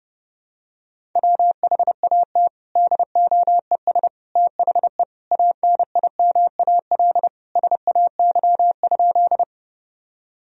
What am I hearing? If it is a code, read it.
Morse code, heard as a keyed tone, about 30 wpm, 700 Hz. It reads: WHAT DOES THE ANIMAL SAY?
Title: Sentences from Top 1000 Words - Review Part 1 - 30wpm